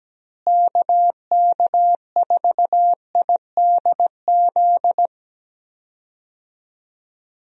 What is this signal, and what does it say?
Morse code, heard as a keyed tone, about 17 wpm, 700 Hz. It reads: KK4IDZ
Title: USA Call Signs - 6 Characters - Repeated 2x - 17wpm